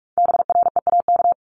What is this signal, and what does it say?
Morse code, heard as a keyed tone, about 46 wpm, 700 Hz. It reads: BREAK
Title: Multi-letter Phonemes Repeated 3x 45wpm